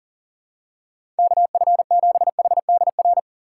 Morse code, heard as a keyed tone, about 40 wpm, 700 Hz. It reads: KF7HDR
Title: USA Call Signs - 6 Characters - 2x Character Spacing - 40wpm